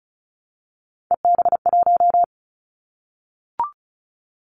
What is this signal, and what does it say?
Morse code, heard as a keyed tone, about 36 wpm, 700 Hz. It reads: EB1
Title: Callsign Prefixes for Top 25 DXCC - 35wpm